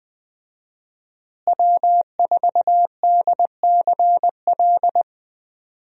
Morse code, heard as a keyed tone, about 20 wpm, 700 Hz. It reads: W4DCL
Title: USA Call Signs - 5 Characters - Repeated 2x - 20wpm